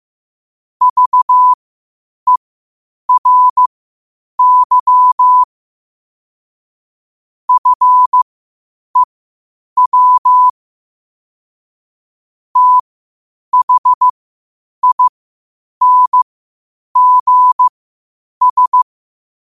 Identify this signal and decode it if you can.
Morse code, heard as a keyed tone, about 15 wpm, spaced as for 8 wpm, 1000 Hz. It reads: VERY FEW THINGS
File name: qotdfiles / 8wpm